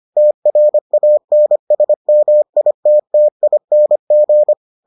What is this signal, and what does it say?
Morse code, heard as a keyed tone, about 25 wpm, 600 Hz. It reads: TRANSMITTING